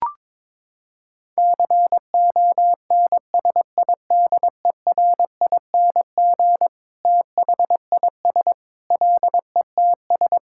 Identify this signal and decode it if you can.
Morse code, heard as a keyed tone, about 22 wpm, 700 Hz. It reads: CONSIDERING THIS LETS
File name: Sentences with 9-Word Phrases - 22wpm